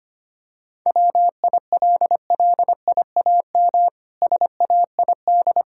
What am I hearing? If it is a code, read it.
Morse code, heard as a keyed tone, about 25 wpm, 700 Hz. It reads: WILLIAM SAID
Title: Pangram Challenge v2 - 25wpm